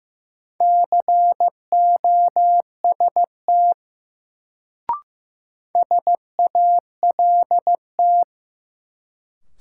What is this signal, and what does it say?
Morse code, heard as a keyed tone, about 15 wpm, 700 Hz. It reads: COST SALT
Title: Sentences from Top 800 Words - New Word Drill - 15wpm